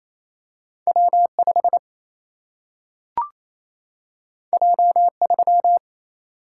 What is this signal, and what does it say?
Morse code, heard as a keyed tone, about 28 wpm, 700 Hz. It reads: W5 J3
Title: Single-Letter Single-Number 28wpm